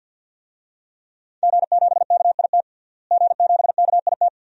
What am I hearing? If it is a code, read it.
Morse code, heard as a keyed tone, about 46 wpm, 700 Hz. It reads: G7KIT G7KIT
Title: DXCC-100 Call Signs - 5 Characters - Repeated 2x - 50wpm